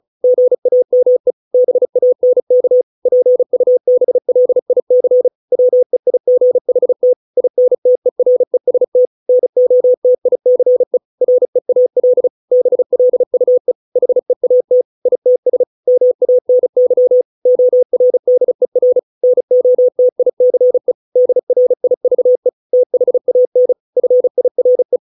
Morse code, heard as a keyed tone, about 34 wpm, 500 Hz. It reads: GAME BANK PUBLIC WEIGHT INTEREST NOTICE REAL BLUE HEAT ITS MANY ORDER NOTICE DRIVE THAN FIRE